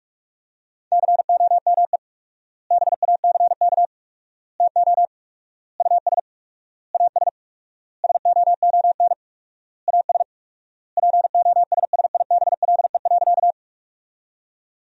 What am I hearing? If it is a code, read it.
Morse code, heard as a keyed tone, about 46 wpm, 690 Hz. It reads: COME BACK TO US AS SOON AS POSSIBLE.
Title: 4x Word-Spacing - Sentences from Top 500 Words - 45wpm